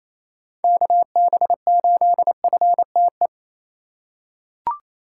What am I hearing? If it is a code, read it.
Morse code, heard as a keyed tone, about 28 wpm, 700 Hz. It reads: KB8FTE
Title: USA Call Signs - 3x Character Spacing - 28wpm